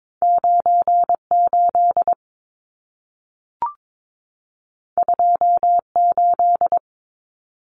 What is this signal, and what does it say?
Morse code, heard as a keyed tone, about 22 wpm, 700 Hz. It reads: 98 28
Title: Numbers - 2-Digit 22wpm